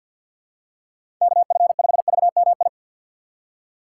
Morse code, heard as a keyed tone, about 47 wpm, 700 Hz. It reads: KF5VMI
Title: USA Call Signs - 6 Characters - 4x Character Spacing - 50wpm